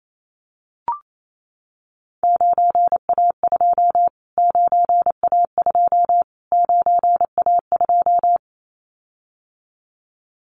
Morse code, heard as a keyed tone, about 28 wpm, 700 Hz. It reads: 9A2 9A2 9A2
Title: Callsign Prefixes for Top 25 DXCC - Repeated 3x - 28wpm